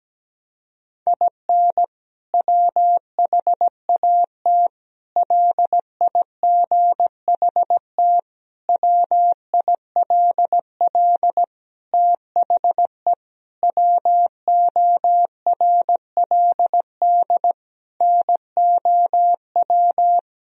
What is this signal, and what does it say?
Morse code, heard as a keyed tone, about 17 wpm, 700 Hz. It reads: IN WHAT LIGHT WILL THE WORLD NOW
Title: Sentences from Top 1000 Words - Review Part 2 - 17wpm